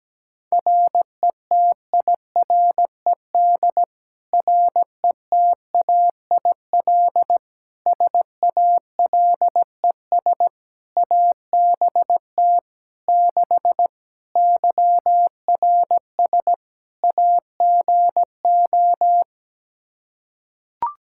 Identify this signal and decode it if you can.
Morse code, heard as a keyed tone, about 17 wpm, 700 Hz. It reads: RETIRED RETAIL SALES ABT 6 YRS AGO
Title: Practice QSO - Exchange 3 - More Info - 17wpm